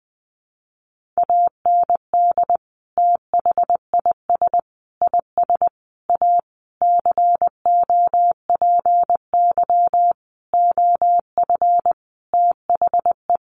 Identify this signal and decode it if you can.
Morse code, heard as a keyed tone, about 20 wpm, 700 Hz. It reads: AND THIS IS A COPY OF THE
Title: Sentences from Top 1400 Words 20wpm